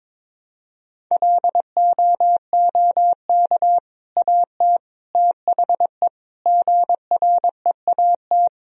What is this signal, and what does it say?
Morse code, heard as a keyed tone, about 22 wpm, 700 Hz. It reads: LOOK AT THE GREAT
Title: Sentences from Top 1000 Words - Review Part 1 - 22wpm